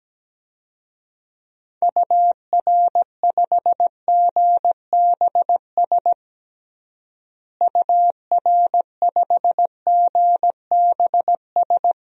Morse code, heard as a keyed tone, about 17 wpm, 700 Hz. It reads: UR5GBS UR5GBS
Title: DXCC-100 Call Signs - 6 Characters - Repeated 2x - 17wpm